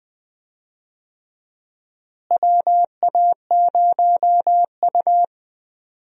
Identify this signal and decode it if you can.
Morse code, heard as a keyed tone, about 20 wpm, 700 Hz. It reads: WA0U
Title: USA Call Signs - 4 Characters - Repeated 3x - 20wpm